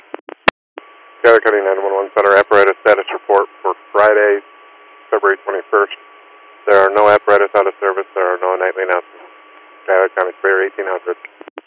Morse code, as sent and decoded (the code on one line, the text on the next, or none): none